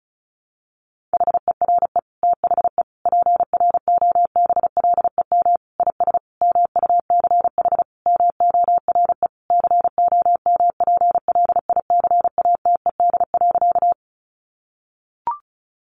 Morse code, heard as a keyed tone, about 34 wpm, 700 Hz. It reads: HERE THE PROBLEM IS MUCH MORE COMPLICATED.
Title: Sentences from Top 1200 Words 35wpm